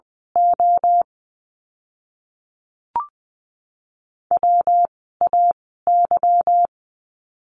Morse code, heard as a keyed tone, about 20 wpm, spaced as for 15 wpm, 700 Hz. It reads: O WAY
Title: 146 Lesson 24f - Letter y - Words 20wpm